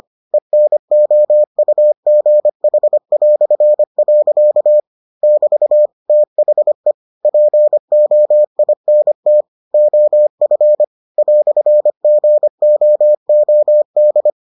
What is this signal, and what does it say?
Morse code, heard as a keyed tone, about 25 wpm, 600 Hz. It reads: ENOUGH". <BT> THE POINT OF "GOOD